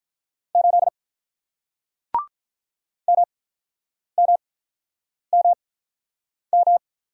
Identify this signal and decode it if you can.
Morse code, heard as a keyed tone, about 48 wpm, spaced as for 9 wpm, 700 Hz. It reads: 8 MMMM